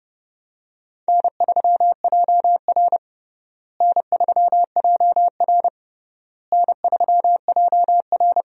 Morse code, heard as a keyed tone, about 30 wpm, 700 Hz. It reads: N3JR N3JR N3JR
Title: USA Call Signs - 4 Characters - Repeated 3x - 30wpm